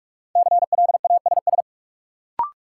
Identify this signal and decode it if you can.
Morse code, heard as a keyed tone, about 46 wpm, 700 Hz. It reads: CLASS
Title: Sentences from Top 1400 Words - New Word Drill Alt - 45wpm